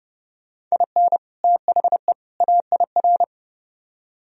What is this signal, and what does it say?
Morse code, heard as a keyed tone, about 39 wpm, 700 Hz. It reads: IN THE AIR